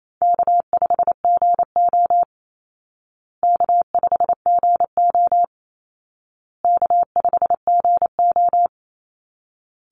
Morse code, heard as a keyed tone, about 28 wpm, 700 Hz. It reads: K5GO K5GO K5GO
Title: Call Signs Repeated 2x - 28wpm